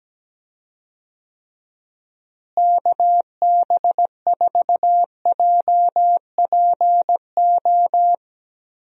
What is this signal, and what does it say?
Morse code, heard as a keyed tone, about 17 wpm, 700 Hz. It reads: KB4JPO
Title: USA Call Signs - 6 Characters - Repeated 3x - 17wpm